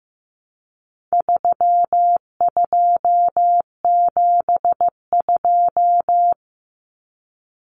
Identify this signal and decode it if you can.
Morse code, heard as a keyed tone, about 15 wpm, 700 Hz. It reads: 3272
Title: Common Cut Numbers - 4-Digit 15wpm